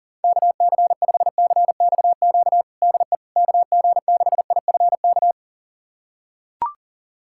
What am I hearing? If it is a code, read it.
Morse code, heard as a keyed tone, about 40 wpm, 700 Hz. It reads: KC5CXQ DE KG6IFK
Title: USA Callsign DE Callsign 40wpm